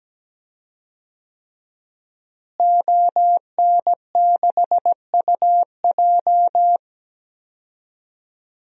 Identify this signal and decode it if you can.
Morse code, heard as a keyed tone, about 17 wpm, 700 Hz. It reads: ON6UJ
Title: DXCC-100 Call Signs - 5 Characters - Repeated 3x - 17wpm